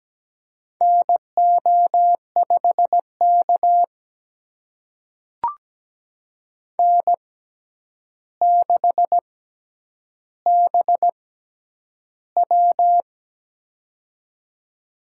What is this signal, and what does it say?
Morse code, heard as a keyed tone, about 17 wpm, spaced as for 9 wpm, 700 Hz. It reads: NO5K N6BW
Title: USA Call Signs - 4 Characters - 6x Character Spacing - 17wpm